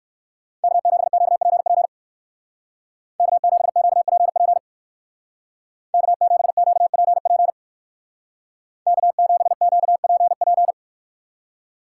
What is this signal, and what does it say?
Morse code, heard as a keyed tone, about 59 wpm, 700 Hz. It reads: K7QPP K7QPP K7QPP K7QPP